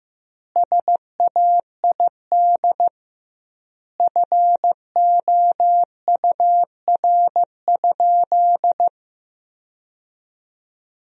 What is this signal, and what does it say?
Morse code, heard as a keyed tone, about 16 wpm, 700 Hz. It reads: SAID FOUR?
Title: 2x Word-Spacing - Sentences from Top 300 Words - 15wpm